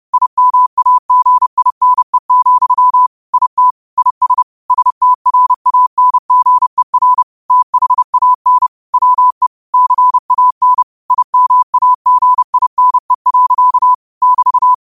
Morse code, heard as a keyed tone, about 30 wpm, 1000 Hz. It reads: IMAGINE, IT IS STRANGER THAN WE CAN IMAGINE. <BT>
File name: qotdfiles / 30wpm